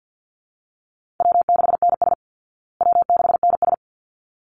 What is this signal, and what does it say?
Morse code, heard as a keyed tone, about 50 wpm, 700 Hz. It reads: W6NS W6NS
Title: Easy Call Signs Repeated 2x - 50wpm